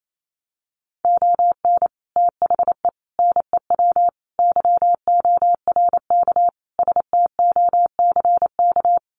Morse code, heard as a keyed tone, about 28 wpm, 700 Hz. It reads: ON THE NEW YORK STOCK
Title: Sets of 5 Words Encore 28wpm